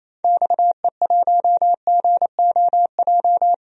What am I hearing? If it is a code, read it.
Morse code, heard as a keyed tone, about 28 wpm, 700 Hz. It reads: XE1GOJ
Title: DXCC-100 Call Signs - 6 Characters - 28wpm